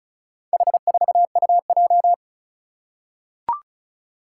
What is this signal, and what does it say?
Morse code, heard as a keyed tone, about 36 wpm, 700 Hz. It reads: H4UJ
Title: Four Challenging Letter-Number Combinations 35wpm